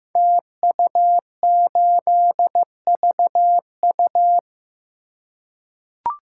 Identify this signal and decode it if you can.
Morse code, heard as a keyed tone, about 15 wpm, 700 Hz. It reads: TU8VU
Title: Code Groups - 5 Letter:Numbers - 15wpm